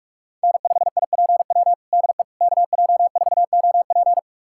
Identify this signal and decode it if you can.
Morse code, heard as a keyed tone, about 46 wpm, 700 Hz. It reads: N5IPW DE KJ4OP
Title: USA Callsign DE Callsign - 5 Characters - 45wpm